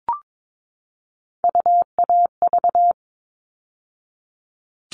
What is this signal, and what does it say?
Morse code, heard as a keyed tone, about 22 wpm, 700 Hz. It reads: UAV
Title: All Cut Numbers - 3-Digit 22wpm